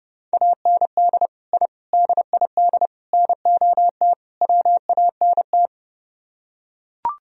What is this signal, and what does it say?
Morse code, heard as a keyed tone, about 29 wpm, 700 Hz. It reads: AND I DID NOT WANT